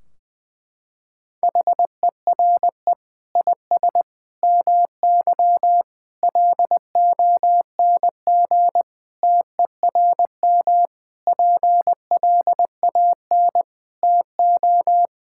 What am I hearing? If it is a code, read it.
Morse code, heard as a keyed tone, about 20 wpm, 700 Hz. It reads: HERE IS MY LONG TERM PLAN TO
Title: Sentences from Top 1600 Words 20wpm